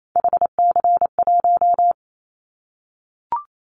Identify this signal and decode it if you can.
Morse code, heard as a keyed tone, about 28 wpm, 700 Hz. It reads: HC1
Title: Callsign Prefixes for Top 100 DXCC - 28wpm